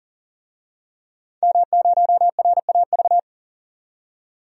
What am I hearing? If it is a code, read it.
Morse code, heard as a keyed tone, about 40 wpm, 700 Hz. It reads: M0RAV
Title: DXCC-100 Call Signs - 5 Characters - 40wpm